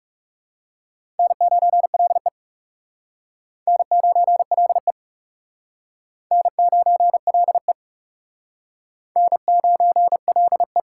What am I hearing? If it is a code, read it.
Morse code, heard as a keyed tone, about 40 wpm, 700 Hz. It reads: N9LE N9LE N9LE N9LE